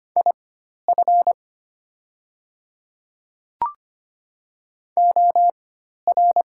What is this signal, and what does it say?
Morse code, heard as a keyed tone, about 25 wpm, spaced as for 12 wpm, 700 Hz. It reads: IF OR
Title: 4x Character Spacing - Top 100 Words - 25wpm